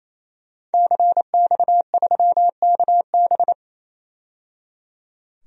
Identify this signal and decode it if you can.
Morse code, heard as a keyed tone, about 28 wpm, 700 Hz. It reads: CX3KB
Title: DXCC-100 Call Signs - 5 Characters - 28wpm